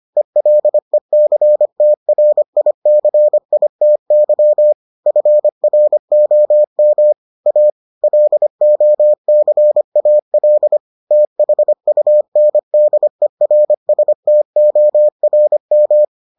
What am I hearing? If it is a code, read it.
Morse code, heard as a keyed tone, about 25 wpm, 600 Hz. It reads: ELECTRICITY FROM A LOCAL THUNDERSTORM